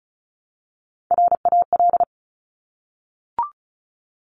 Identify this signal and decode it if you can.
Morse code, heard as a keyed tone, about 36 wpm, 700 Hz. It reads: RAL
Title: Top Three Letter Combinations in Words 35wpm